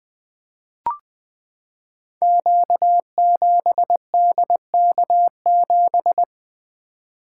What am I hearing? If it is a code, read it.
Morse code, heard as a keyed tone, about 20 wpm, 700 Hz. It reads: Q7DK7
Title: Code Groups - 5 Letter:Numbers - 20wpm